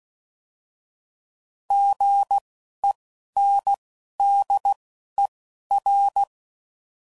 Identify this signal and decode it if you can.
Morse code, heard as a keyed tone, about 16 wpm, spaced as for 11 wpm, 800 Hz. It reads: GENDER
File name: Prefixes / By First Letter